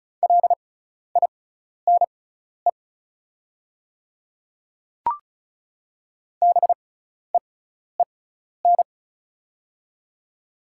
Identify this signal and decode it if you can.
Morse code, heard as a keyed tone, about 36 wpm, spaced as for 12 wpm, 690 Hz. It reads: LINE BEEN
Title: 6x Character Spacing - Top 300 Words - 35wpm